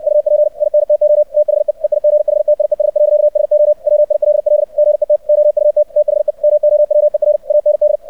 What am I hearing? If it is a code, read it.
Morse code, heard as a keyed tone, about 64 wpm, 600 Hz. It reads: UP INTO THE EIGHTIES. SO WILL GET OUT THE MOWER AND
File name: archived / CRSnet